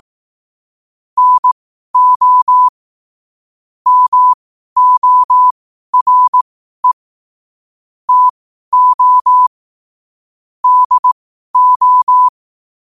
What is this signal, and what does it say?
Morse code, heard as a keyed tone, about 18 wpm, spaced as for 12 wpm, 1000 Hz. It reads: NO MORE TO DO
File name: qotdfiles / 12wpm